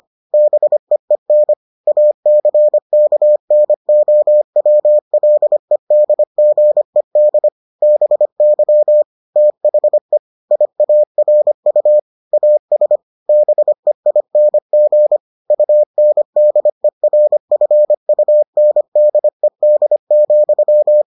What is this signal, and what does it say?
Morse code, heard as a keyed tone, about 25 wpm, 600 Hz. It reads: BEEN ACKNOWLEDGED BY THE IARU AS BEING UNDERFUNDED,